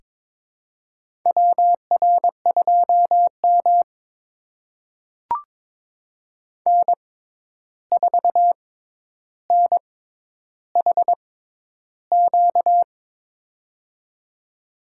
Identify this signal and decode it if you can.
Morse code, heard as a keyed tone, about 22 wpm, spaced as for 11 wpm, 700 Hz. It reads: WR2M N4NHQ
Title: USA Call Signs - 6x Character Spacing - 22wpm